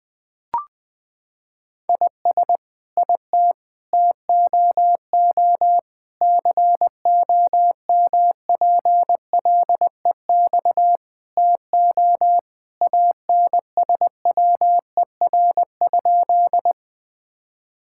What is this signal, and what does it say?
Morse code, heard as a keyed tone, about 21 wpm, 700 Hz. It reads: IS IT TOO COMPLEX TO ANSWER?
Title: Sentences from Top 1000 Words - Review Part 6 - 20wpm